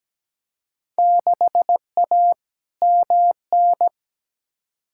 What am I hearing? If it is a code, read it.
Morse code, heard as a keyed tone, about 17 wpm, 700 Hz. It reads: 6A MN
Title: Field Day Exchanges Encore - 17wpm